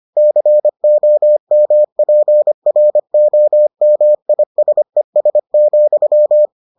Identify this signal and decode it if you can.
Morse code, heard as a keyed tone, about 25 wpm, 600 Hz. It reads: COMPROMISES,